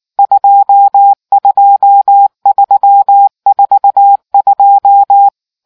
Morse code, heard as a keyed tone, about 19 wpm, 800 Hz. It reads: 22342